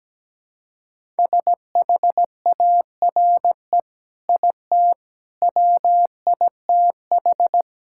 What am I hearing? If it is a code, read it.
Morse code, heard as a keyed tone, about 17 wpm, 700 Hz. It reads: SHARE IT WITH
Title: Sentences from Top 1000 Words - Review Part 6 - 17wpm